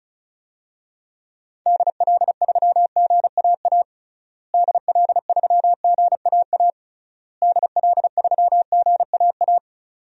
Morse code, heard as a keyed tone, about 34 wpm, 700 Hz. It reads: DL3GAA DL3GAA DL3GAA
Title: DXCC-100 Call Signs - 6 Characters - Repeated 3x - 35wpm